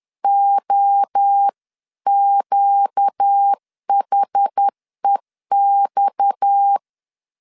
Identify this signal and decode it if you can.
Morse code, heard as a keyed tone, about 10 wpm, 800 Hz. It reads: OQHEX